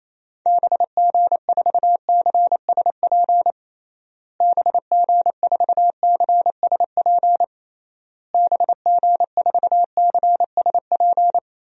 Morse code, heard as a keyed tone, about 27 wpm, 700 Hz. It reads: BG4CSP BG4CSP BG4CSP
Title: DXCC-100 Call Signs - 6 Characters - Repeated 3x - 28wpm